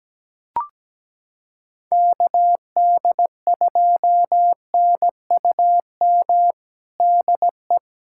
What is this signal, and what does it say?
Morse code, heard as a keyed tone, about 17 wpm, 700 Hz. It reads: KD2NUM DE
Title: USA Callsign DE Callsign 17wpm